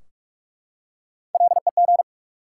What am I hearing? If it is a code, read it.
Morse code, heard as a keyed tone, about 44 wpm, 700 Hz. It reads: LEG
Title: Three Letter Words - 45wpm